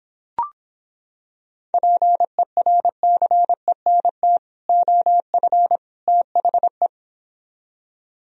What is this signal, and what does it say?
Morse code, heard as a keyed tone, about 26 wpm, 700 Hz. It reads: PERCENT OF THE